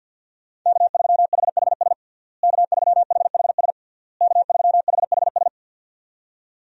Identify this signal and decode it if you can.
Morse code, heard as a keyed tone, about 52 wpm, 700 Hz. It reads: K3HHS K3HHS K3HHS
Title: USA Call Signs - 5 Characters - Repeated 3x - 50wpm